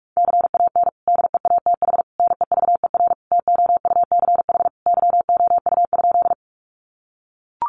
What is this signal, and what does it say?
Morse code, heard as a keyed tone, about 44 wpm, 700 Hz. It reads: CAN DEATH NEVER TOUCH YOU?
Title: Sentences from Top 1200 Words 45wpm